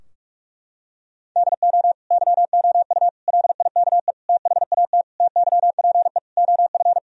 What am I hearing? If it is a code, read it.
Morse code, heard as a keyed tone, about 46 wpm, 700 Hz. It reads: DO YOU LIKE THAT TYPE OF
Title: Sentences from Top 1000 Words - Review Part 3 - 45wpm